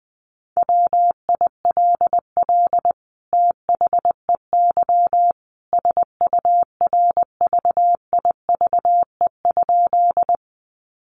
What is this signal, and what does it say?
Morse code, heard as a keyed tone, about 20 wpm, 700 Hz. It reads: WILL THEY SURVIVE?
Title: Sentences from Top 1300 Words 20wpm